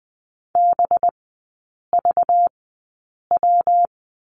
Morse code, heard as a keyed tone, about 20 wpm, spaced as for 8 wpm, 700 Hz. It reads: BVW
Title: 191 Lesson 31e - Letter v - Three Characters 20wpm